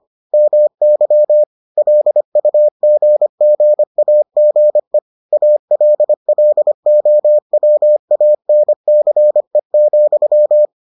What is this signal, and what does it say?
Morse code, heard as a keyed tone, about 25 wpm, 600 Hz. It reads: MY LUGGAGE ALLOWANCE,